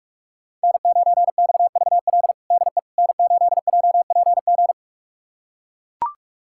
Morse code, heard as a keyed tone, about 41 wpm, 700 Hz. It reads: N9XVL DE N8JPG
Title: USA Callsign DE Callsign - 5 Characters - 45wpm